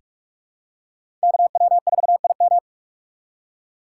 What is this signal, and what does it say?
Morse code, heard as a keyed tone, about 46 wpm, 700 Hz. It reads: KW4IM
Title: USA Call Signs - 5 Characters - 3x Character Spacing - 45wpm